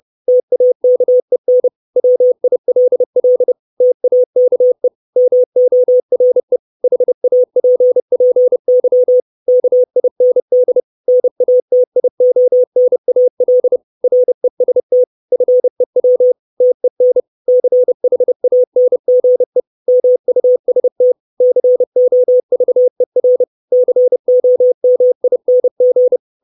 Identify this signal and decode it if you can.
Morse code, heard as a keyed tone, about 30 wpm, 500 Hz. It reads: TAKEN WILL TAKE MORE HAPPY KIND NATIONAL REST FEW TEN CHANGE MUST COVER COMING
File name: 632 Words 30WPM